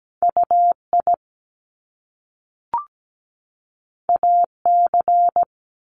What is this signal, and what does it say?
Morse code, heard as a keyed tone, about 17 wpm, 700 Hz. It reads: UI AC
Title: Top Two Letter Combinations in Words 17wpm